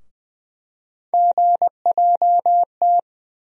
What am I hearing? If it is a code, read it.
Morse code, heard as a keyed tone, about 20 wpm, 700 Hz. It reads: GJT
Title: USA Call Sign Postfixes - 3 Letters 20wpm